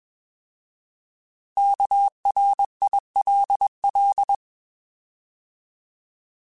Morse code, heard as a keyed tone, about 21 wpm, 800 Hz. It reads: KRILL
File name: Alphabetical / Alphabetical mp3s / Ks @20